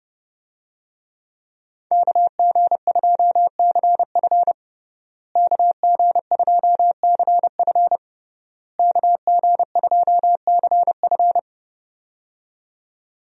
Morse code, heard as a keyed tone, about 28 wpm, 700 Hz. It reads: KG2CF KG2CF KG2CF
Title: USA Call Signs - 5 Characters - Repeated 3x - 30wpm